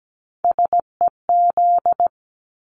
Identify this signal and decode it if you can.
Morse code, heard as a keyed tone, about 17 wpm, 700 Hz. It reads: SEZ
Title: Abbreviations 17wpm